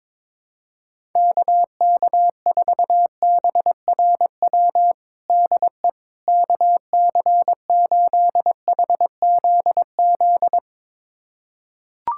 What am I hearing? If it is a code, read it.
Morse code, heard as a keyed tone, about 22 wpm, 700 Hz. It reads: KK4BRW DE KC8HZZ
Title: USA Callsign DE Callsign - 6 Characters - 22wpm